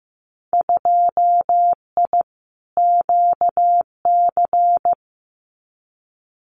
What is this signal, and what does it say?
Morse code, heard as a keyed tone, about 15 wpm, 700 Hz. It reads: 2I QC
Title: Winter Field Day Exchanges 15wpm